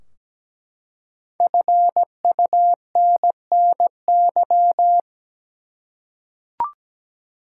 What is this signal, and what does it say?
Morse code, heard as a keyed tone, about 17 wpm, 700 Hz. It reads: FUNNY
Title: Five Letter Words - 17wpm